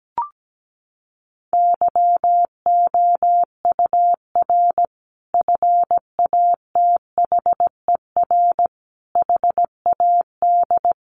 Morse code, heard as a keyed tone, about 17 wpm, 700 Hz. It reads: YOUR FATHER HAD
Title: Sentences from Top 1100 Words 17wpm